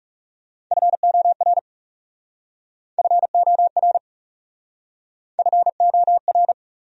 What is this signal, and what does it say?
Morse code, heard as a keyed tone, about 40 wpm, 700 Hz. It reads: FOR FOR FOR